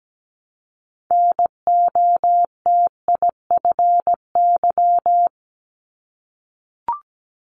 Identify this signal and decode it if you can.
Morse code, heard as a keyed tone, about 17 wpm, 700 Hz. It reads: NOTIFY
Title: Multi-letter Phonemes 17wpm